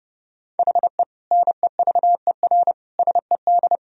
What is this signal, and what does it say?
Morse code, heard as a keyed tone, about 30 wpm, 700 Hz. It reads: HE NEVER SED
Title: Sentences from Top 500 Words and Abbreviations 30wpm